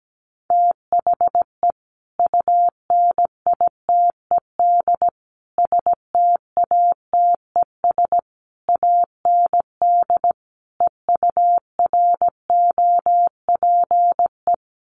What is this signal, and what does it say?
Morse code, heard as a keyed tone, about 17 wpm, 700 Hz. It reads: THE UNITED STATES AND EUROPE
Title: Sets of 5 Words Encore 17wpm